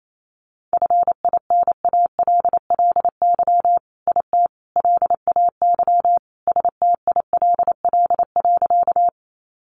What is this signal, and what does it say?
Morse code, heard as a keyed tone, about 28 wpm, 700 Hz. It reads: FINALLY IT LAY STILL.
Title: Sentences from Top 1000 Words 28wpm